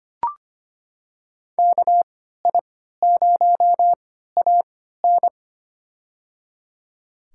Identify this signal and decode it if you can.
Morse code, heard as a keyed tone, about 25 wpm, spaced as for 14 wpm, 700 Hz. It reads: KI0AN
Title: USA Call Signs - 5 Characters - 3x Character Spacing - 25wpm